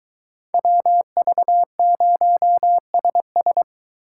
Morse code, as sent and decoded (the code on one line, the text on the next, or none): .-- ...- ----- ... ...
WV0SS